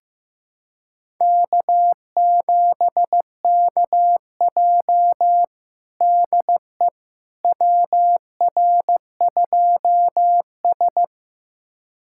Morse code, heard as a keyed tone, about 15 wpm, 700 Hz. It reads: K7KJ DE WR2S
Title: USA Callsign DE Callsign - 4 Characters - 15wpm